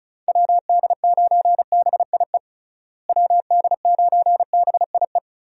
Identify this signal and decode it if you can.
Morse code, heard as a keyed tone, about 34 wpm, 700 Hz. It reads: WD9BIE WD9BIE
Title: USA Call Signs - 6 Characters - Repeated 3x - 35wpm